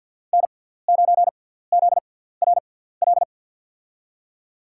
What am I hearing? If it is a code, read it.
Morse code, heard as a keyed tone, about 53 wpm, spaced as for 17 wpm, 700 Hz. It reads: N9ZRL
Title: USA Call Signs - 5 Characters - 6x Character Spacing - 50wpm